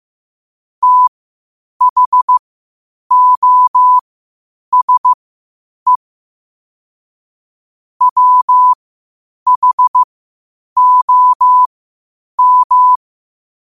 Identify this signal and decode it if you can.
Morse code, heard as a keyed tone, about 15 wpm, spaced as for 8 wpm, 1000 Hz. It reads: THOSE WHOM
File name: qotdfiles / 8wpm